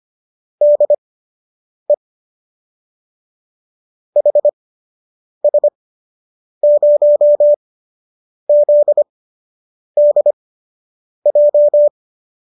Morse code, heard as a keyed tone, about 25 wpm, spaced as for 8 wpm, 590 Hz. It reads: DE HS0ZDJ